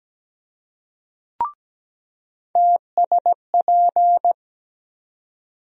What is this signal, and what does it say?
Morse code, heard as a keyed tone, about 17 wpm, 700 Hz. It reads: TSP
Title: USA Call Sign Postfixes - 3 Letters 17wpm